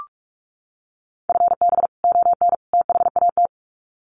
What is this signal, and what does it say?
Morse code, heard as a keyed tone, about 44 wpm, 700 Hz. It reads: FB ON THAT
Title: Common QSO Phrases 45wpm